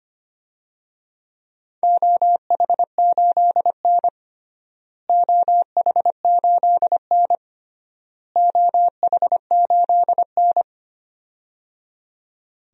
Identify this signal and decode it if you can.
Morse code, heard as a keyed tone, about 25 wpm, 700 Hz. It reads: OH8N OH8N OH8N
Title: DXCC-100 Call Signs - 4 Characters - Repeated 3x - 25wpm